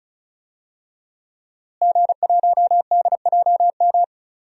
Morse code, heard as a keyed tone, about 34 wpm, 700 Hz. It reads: G1DJM
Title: DXCC-100 Call Signs - 5 Characters - Repeated 2x - 35wpm